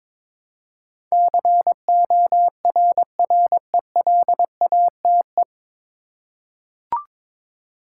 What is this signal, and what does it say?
Morse code, heard as a keyed tone, about 22 wpm, 700 Hz. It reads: CORRELATE
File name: 9 Letter Words - First 3 Letters Given - 22wpm